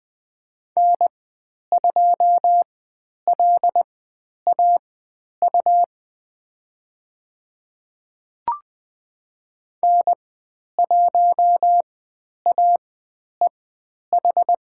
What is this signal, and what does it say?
Morse code, heard as a keyed tone, about 20 wpm, spaced as for 10 wpm, 700 Hz. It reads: N2LAU N1AEH
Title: 109 Lesson 18g - Letter m - Call Signs 20wpm